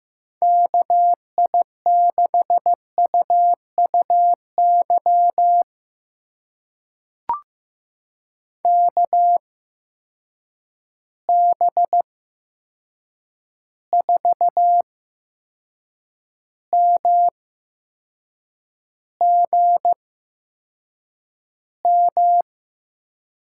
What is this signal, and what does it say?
Morse code, heard as a keyed tone, about 15 wpm, spaced as for 6 wpm, 700 Hz. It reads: KI6UUY KB4MGM
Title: USA Call Signs - 6 Characters - 8x Character Spacing - 15wpm